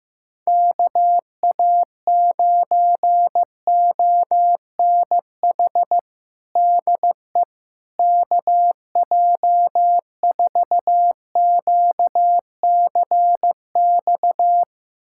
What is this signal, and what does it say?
Morse code, heard as a keyed tone, about 15 wpm, 700 Hz. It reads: KA9ONH DE KJ4QCX
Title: USA Callsign DE Callsign - 6 Characters - 15wpm